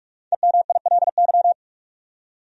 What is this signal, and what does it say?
Morse code, heard as a keyed tone, about 46 wpm, 700 Hz. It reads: EMILY
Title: USA Common Names - Max 5 Letters - 45wpm